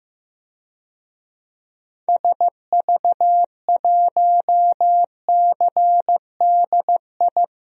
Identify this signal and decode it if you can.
Morse code, heard as a keyed tone, about 15 wpm, 700 Hz. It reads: SV1CDI
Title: DXCC-100 Call Signs - 6 Characters - 15wpm